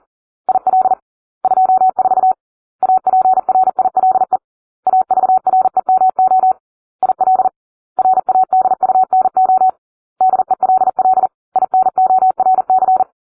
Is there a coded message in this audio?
Yes: Morse code from 0.5 to 13.0 s, heard as a keyed tone, about 40 wpm, 750 Hz. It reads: IL 24 APRILE AVREMO IL RADUNO DELL INORC